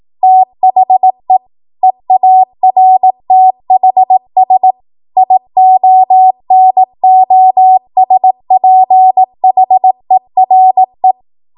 Morse code, heard as a keyed tone, about 18 wpm, 750 Hz. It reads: THE EARTHS IONOSPHERE
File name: Morse / 18WPM